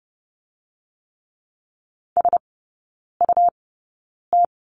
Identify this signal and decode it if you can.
Morse code, heard as a keyed tone, about 30 wpm, spaced as for 9 wpm, 690 Hz. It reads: SUT
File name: Lesson 17e - Letter u - Three Characters 30wpm